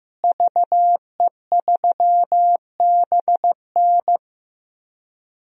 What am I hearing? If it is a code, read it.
Morse code, heard as a keyed tone, about 15 wpm, 700 Hz. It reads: VE3BN
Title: DXCC-100 Call Signs - 5 Characters - Repeated 2x - 15wpm